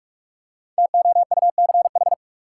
Morse code, heard as a keyed tone, about 42 wpm, 700 Hz. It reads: TOUCH
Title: Five Letter Words - 45wpm